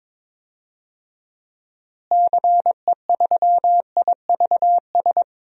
Morse code, heard as a keyed tone, about 22 wpm, 700 Hz. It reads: CE3IVS
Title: DXCC-100 Call Signs - 6 Characters - Repeated 2x - 22wpm